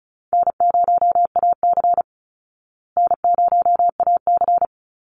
Morse code, heard as a keyed tone, about 34 wpm, 700 Hz. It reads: N0AC N0AC
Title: Easy Call Signs Repeated 2x - 35wpm